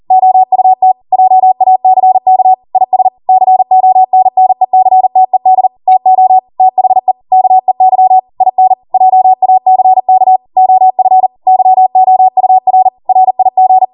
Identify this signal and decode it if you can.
Morse code, heard as a keyed tone, about 40 wpm, 750 Hz. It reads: OUT JACK IS CONNECTED TO THE KEY IN JACK OF YOUR RIG